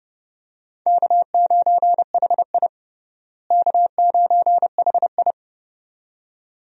Morse code, heard as a keyed tone, about 30 wpm, 700 Hz. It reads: K9HI K9HI
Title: USA Call Signs - 4 Characters - Repeated 2x - 30wpm